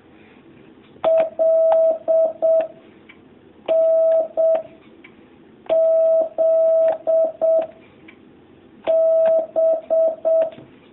Morse code, heard as a keyed tone, about 7 wpm, 650 Hz. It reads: L N Z B
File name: jh8cbh / 2019cw